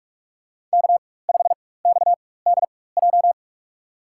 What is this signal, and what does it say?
Morse code, heard as a keyed tone, about 46 wpm, spaced as for 21 wpm, 700 Hz. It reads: K5XDJ
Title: USA Call Signs - 4x Character Spacing - 45wpm